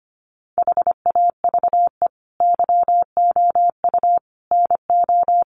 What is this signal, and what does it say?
Morse code, heard as a keyed tone, about 25 wpm, 700 Hz. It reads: HAVE YOU NO
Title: Sentences from Top 1200 Words 25wpm